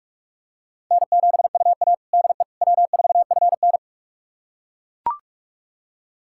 Morse code, heard as a keyed tone, about 41 wpm, 700 Hz. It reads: N7UA DE W4FN
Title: USA Callsign DE Callsign - 4 Characters - 45wpm